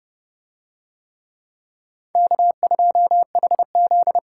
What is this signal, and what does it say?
Morse code, heard as a keyed tone, about 28 wpm, 700 Hz. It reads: K2HZ